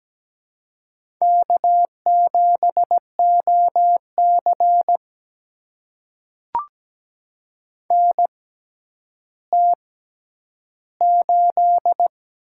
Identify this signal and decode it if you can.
Morse code, heard as a keyed tone, about 17 wpm, spaced as for 10 wpm, 700 Hz. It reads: K7OC NT8
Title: USA Call Signs - 4 Characters - 6x Character Spacing - 17wpm